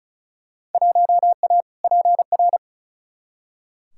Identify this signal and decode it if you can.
Morse code, heard as a keyed tone, about 34 wpm, 700 Hz. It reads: 1A PR
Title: Field Day Exchanges Encore - 35wpm